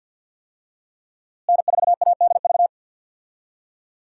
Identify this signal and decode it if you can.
Morse code, heard as a keyed tone, about 46 wpm, 700 Hz. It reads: N4ADV
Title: USA Call Signs - 5 Characters - Repeated 3x - 50wpm